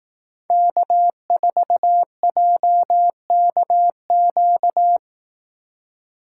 Morse code, heard as a keyed tone, about 18 wpm, 700 Hz. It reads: K4JKQ
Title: Speed Racing - USA 1x3 Call Signs 15wpm